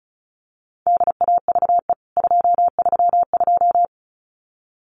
Morse code, heard as a keyed tone, about 34 wpm, 700 Hz. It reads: DAVE 232
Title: CWT Exchange 35wpm